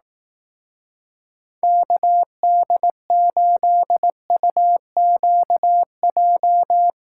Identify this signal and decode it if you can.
Morse code, heard as a keyed tone, about 18 wpm, 700 Hz. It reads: KD8UQJ
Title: Speed Racing - USA 2x3 Call Signs 15wpm